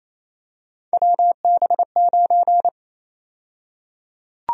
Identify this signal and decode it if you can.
Morse code, heard as a keyed tone, about 27 wpm, 700 Hz. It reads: WB9
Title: USA Call Sign Prefixes - 28wpm